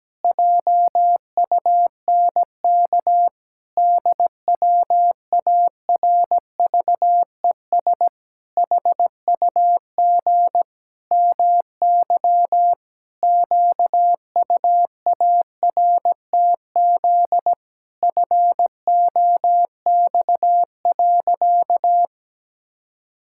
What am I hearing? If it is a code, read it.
Morse code, heard as a keyed tone, about 17 wpm, 700 Hz. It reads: JUNK DWARVES HUG MY QUARTZ FOX.
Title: Pangram Challenge v2 - 17wpm